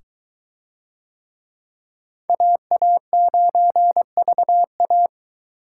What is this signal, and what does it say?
Morse code, heard as a keyed tone, about 22 wpm, 700 Hz. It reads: AA9VA